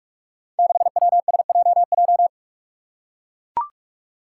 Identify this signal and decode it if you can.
Morse code, heard as a keyed tone, about 42 wpm, 700 Hz. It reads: 6WSJJ
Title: Five Challenging Letter-Number Combinations 45wpm